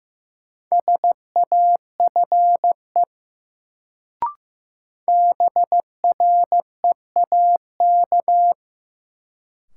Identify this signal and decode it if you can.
Morse code, heard as a keyed tone, about 15 wpm, 700 Hz. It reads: SAFE BREAK
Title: Sentences from Top 900 Words - New Word Drill - 15wpm